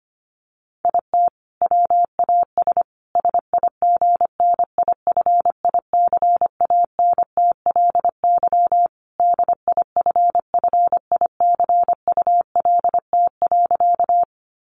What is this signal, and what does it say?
Morse code, heard as a keyed tone, about 25 wpm, 700 Hz. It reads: IT WAS SIGNIFICANTLY DIFFICULT.
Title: Sentences from Top 1500 Words 25wpm